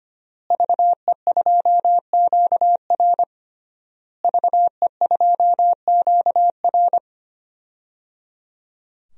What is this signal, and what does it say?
Morse code, heard as a keyed tone, about 25 wpm, 700 Hz. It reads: VE2QR VE2QR
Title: DXCC-100 Call Signs - 5 Characters - Repeated 3x - 25wpm